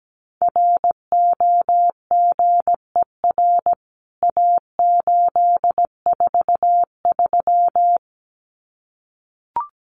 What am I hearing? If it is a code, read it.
Morse code, heard as a keyed tone, about 17 wpm, 700 Hz. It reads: ROGER A843
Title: CWT Exchange 17wpm